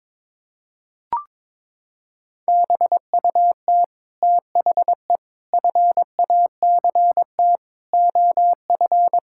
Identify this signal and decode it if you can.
Morse code, heard as a keyed tone, about 22 wpm, 700 Hz. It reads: BUT THE FACT OF
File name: Sets of 9 Words - 22wpm